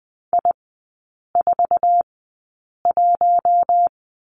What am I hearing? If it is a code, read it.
Morse code, heard as a keyed tone, about 20 wpm, spaced as for 8 wpm, 700 Hz. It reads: I41
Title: 047 Lesson 9e - Number 4 - Three Characters 20wpm